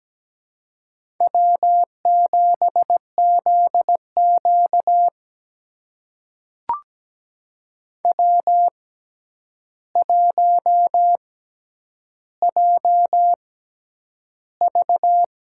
Morse code, heard as a keyed tone, about 17 wpm, spaced as for 9 wpm, 700 Hz. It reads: W7ZQ W1JV
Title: USA Call Signs - 4 Characters - 6x Character Spacing - 17wpm